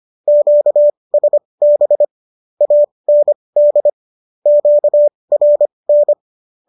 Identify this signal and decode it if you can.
Morse code, heard as a keyed tone, about 25 wpm, spaced as for 20 wpm, 590 Hz. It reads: QSB AND QRN